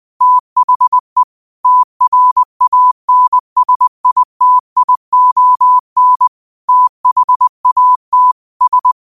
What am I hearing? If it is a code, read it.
Morse code, heard as a keyed tone, about 20 wpm, 1000 Hz. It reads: THE TRANSITION THAT S
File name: qotdfiles / 20wpm